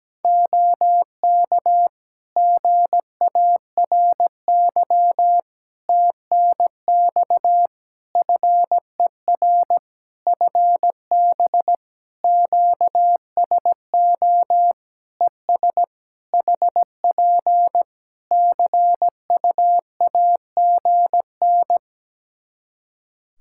Morse code, heard as a keyed tone, about 17 wpm, 700 Hz. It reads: OK GARY TNX FER FB QSO ES HP CUAGN
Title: Practice QSO - Exchange 4 - Ending - 17wpm